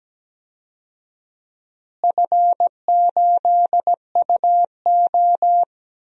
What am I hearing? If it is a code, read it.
Morse code, heard as a keyed tone, about 17 wpm, 700 Hz. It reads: F8UO
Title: DXCC-100 Call Signs - 4 Characters - Repeated 2x - 17wpm